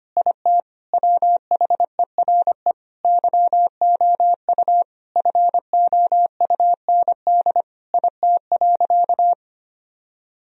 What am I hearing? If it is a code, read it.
Morse code, heard as a keyed tone, about 25 wpm, 700 Hz. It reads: IT WHERE YOU FOUND IT.
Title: Sentences from Top 1000 Words - Review Part 4 - 25wpm